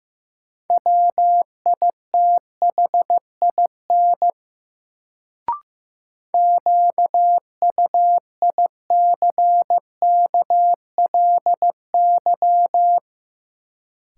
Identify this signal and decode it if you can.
Morse code, heard as a keyed tone, about 15 wpm, 700 Hz. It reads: WITHIN QUICKLY
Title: Sentences from Top 700 Words - New Word Drill - 15wpm